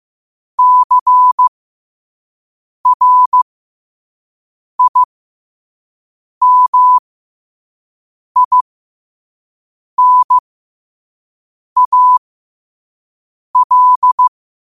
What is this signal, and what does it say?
Morse code, heard as a keyed tone, about 15 wpm, spaced as for 5 wpm, 1000 Hz. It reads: CRIMINAL
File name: qotdfiles / 5wpm